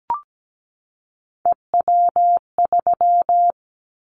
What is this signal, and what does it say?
Morse code, heard as a keyed tone, about 17 wpm, 700 Hz. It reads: EW3
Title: Callsign Prefixes for Top 100 DXCC - 17wpm